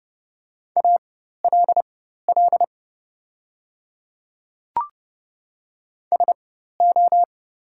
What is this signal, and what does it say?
Morse code, heard as a keyed tone, about 30 wpm, spaced as for 14 wpm, 700 Hz. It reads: ALL SO
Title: 4x Character Spacing - Top 100 Words - 30wpm